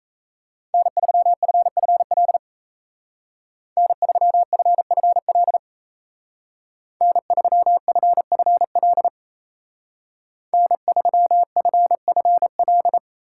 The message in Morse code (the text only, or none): N3FFL N3FFL N3FFL N3FFL